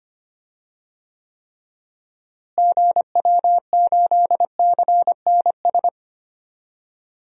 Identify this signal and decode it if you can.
Morse code, heard as a keyed tone, about 25 wpm, 700 Hz. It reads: GW8CNS
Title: DXCC-100 Call Signs - 6 Characters - Repeated 2x - 25wpm